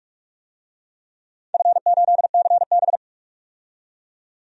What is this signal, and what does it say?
Morse code, heard as a keyed tone, about 46 wpm, 700 Hz. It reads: F8CB
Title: DXCC-100 Call Signs - 4 Characters - 45wpm